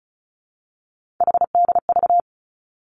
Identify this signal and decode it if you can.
Morse code, heard as a keyed tone, about 36 wpm, 700 Hz. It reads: HDV
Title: Three Challenging Letter-Number Combinations 35wpm